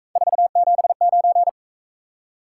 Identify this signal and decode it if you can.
Morse code, heard as a keyed tone, about 41 wpm, 700 Hz. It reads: 479